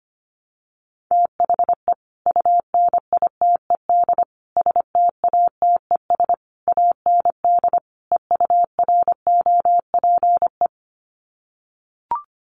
Morse code, heard as a keyed tone, about 25 wpm, 700 Hz. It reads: THE UNITED STATES AND EUROPE
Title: Sets of 5 Words Encore 25wpm